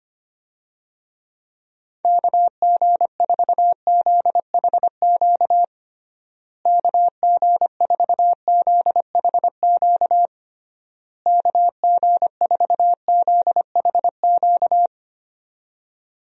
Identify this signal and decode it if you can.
Morse code, heard as a keyed tone, about 25 wpm, 700 Hz. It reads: KG4ZHQ KG4ZHQ KG4ZHQ
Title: USA Call Signs - 6 Characters - Repeated 3x - 25wpm